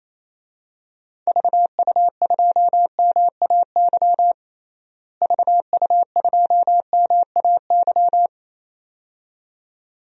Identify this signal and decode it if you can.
Morse code, heard as a keyed tone, about 27 wpm, 700 Hz. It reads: VU2MAY VU2MAY
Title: DXCC-100 Call Signs - 6 Characters - Repeated 2x - 28wpm